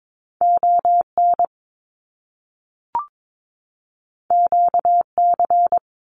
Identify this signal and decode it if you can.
Morse code, heard as a keyed tone, about 22 wpm, 700 Hz. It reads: ON QC
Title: Canadian Provinces and Territories 22wpm